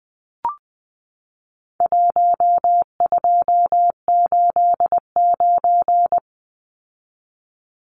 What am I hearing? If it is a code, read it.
Morse code, heard as a keyed tone, about 20 wpm, 700 Hz. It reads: 1289
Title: Numbers - Contest Style - 4-Digit 20wpm